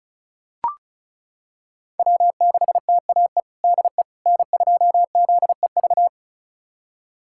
Morse code, heard as a keyed tone, about 36 wpm, 700 Hz. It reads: W6TAE DE N2ZEV
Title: USA Callsign DE Callsign - 5 Characters - 35wpm